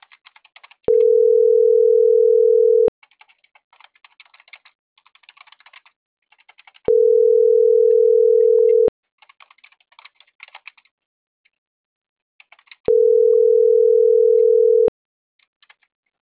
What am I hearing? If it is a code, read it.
Ringback tone (North America), 440 + 480 Hz.
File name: CENTRAL NAVIERAS / out roberto.rueda